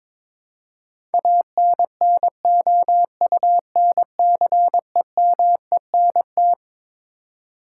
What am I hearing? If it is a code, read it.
Morse code, heard as a keyed tone, about 22 wpm, 700 Hz. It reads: ANNOUNCEMENT
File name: Twelve Letter Words - 22wpm